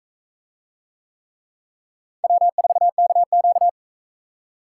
Morse code, heard as a keyed tone, about 43 wpm, 700 Hz. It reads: W4KQ